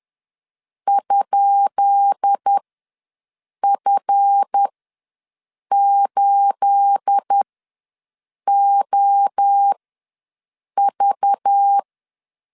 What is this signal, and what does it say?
Morse code, heard as a keyed tone, about 10 wpm, spaced as for 6 wpm, 800 Hz. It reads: ?F8OV
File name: LEZIONI / 10 SIGNS / 10B INT ws 10wpm 4134